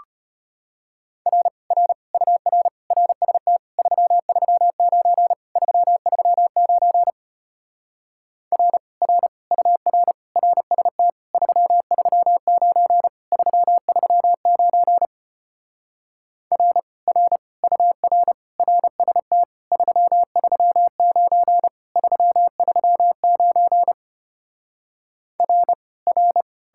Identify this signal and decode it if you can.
Morse code, heard as a keyed tone, about 33 wpm, 700 Hz. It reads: R R UR RST 339 339 R R UR RST 339 339 R R UR RST 339 339 R R